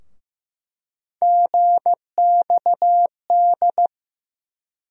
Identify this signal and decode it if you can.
Morse code, heard as a keyed tone, about 15 wpm, 700 Hz. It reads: GXD
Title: USA Call Sign Postfixes - 3 Letters 15wpm